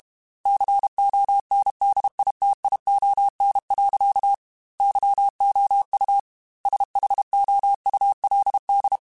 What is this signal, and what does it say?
Morse code, heard as a keyed tone, about 32 wpm, 800 Hz. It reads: CONDITION. YOU SHOULD
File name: Technical Subjects / Technical Subjects-mp3s / OR Gate @30